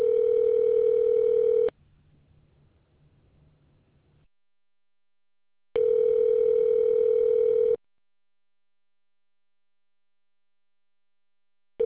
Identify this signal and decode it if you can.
Ringback tone (North America), 440 + 480 Hz.